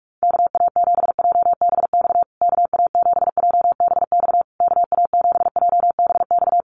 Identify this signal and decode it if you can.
Morse code, heard as a keyed tone, about 44 wpm, 690 Hz. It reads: KA7JBX KA7JBX KA7JBX
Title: USA Call Signs Repeated 3x 45wpm